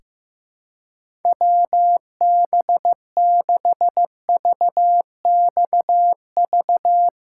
Morse code, heard as a keyed tone, about 15 wpm, 700 Hz. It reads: WB6VXV
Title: USA Callsign DE Callsign 15wpm